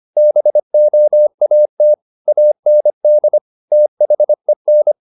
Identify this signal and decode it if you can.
Morse code, heard as a keyed tone, about 25 wpm, 600 Hz. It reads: BOAT AND THEN
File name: the old man and the sea part 1 25wpm